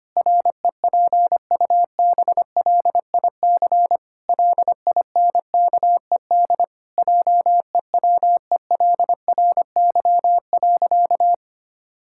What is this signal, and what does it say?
Morse code, heard as a keyed tone, about 25 wpm, 700 Hz. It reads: REPUBLIC LINKED JEWELRY.
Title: Pangram Challenge v2 - 25wpm